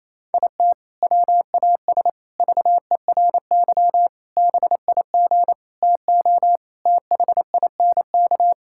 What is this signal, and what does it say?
Morse code, heard as a keyed tone, about 28 wpm, 700 Hz. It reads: IT WAS VERY BIG TO THINK
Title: Sentences from Top 1600 Words 28wpm